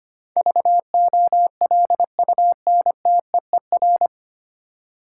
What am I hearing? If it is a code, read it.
Morse code, heard as a keyed tone, about 25 wpm, 700 Hz. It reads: VOLUNTEER
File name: Nine Letter Words - 25wpm